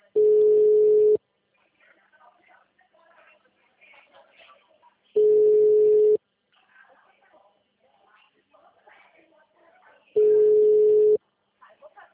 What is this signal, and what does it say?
Ringback tone (Europe, 425 Hz), with speech.